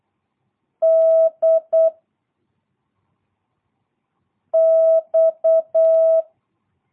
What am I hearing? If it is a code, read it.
Morse code, heard as a keyed tone, about 8 wpm, 660 Hz. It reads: D X
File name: jh8cbh / 2022cw